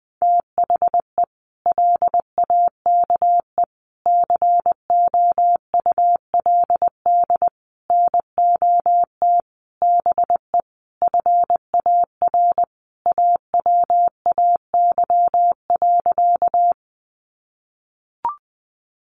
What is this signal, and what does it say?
Morse code, heard as a keyed tone, about 20 wpm, 700 Hz. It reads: THE LAKE COULD NOT BE FAR AWAY.
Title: Sentences from Top 1100 Words 20wpm